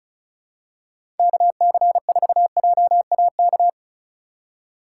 Morse code, heard as a keyed tone, about 36 wpm, 700 Hz. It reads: KC4JAK
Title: USA Call Signs - 4x Character Spacing - 35wpm